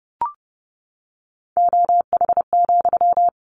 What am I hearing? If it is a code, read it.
Morse code, heard as a keyed tone, about 30 wpm, 700 Hz. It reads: OH,
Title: Sentences from Top 1100 Words 30wpm